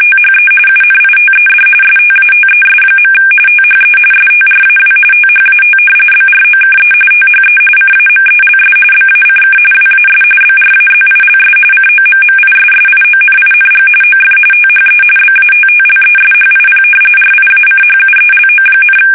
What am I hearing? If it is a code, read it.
RTTY, about 45.45 baud. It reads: THE UNITED STATES. THE STRAW THAT BROKE THE CAMELS BACK WAS BRAZILS COMMISSIONING OF THE BATTLESHIP RIACHUELO,
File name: fsk850 / battleships